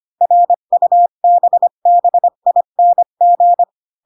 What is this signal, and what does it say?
Morse code, heard as a keyed tone, about 25 wpm, 700 Hz. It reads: RUBBING
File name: Pinocchio / Pinocchio ch1 20 25 ik1hge